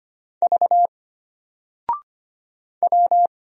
Morse code, heard as a keyed tone, about 25 wpm, 700 Hz. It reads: V W
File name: Single-Letter-Number-25wpm